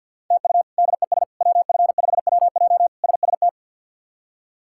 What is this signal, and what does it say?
Morse code, heard as a keyed tone, about 47 wpm, 700 Hz. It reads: TU DES WF5WJ SST
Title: K1USN SST QSO Practice - 50wpm